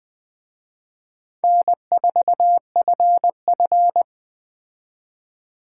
Morse code, heard as a keyed tone, about 20 wpm, 700 Hz. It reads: N4FF
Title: Speed Racing - USA 1x2 Call Signs 17wpm